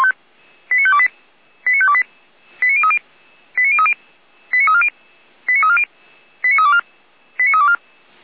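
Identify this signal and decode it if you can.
Five-tone selective call (ZVEI), heard as quick runs of five tones, tones of about 70 ms. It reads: 89528 89528 89030 89030 89340 89340 89324 89324
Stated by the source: Received by SDR; ZVEI five-tone call 89503, channel 455.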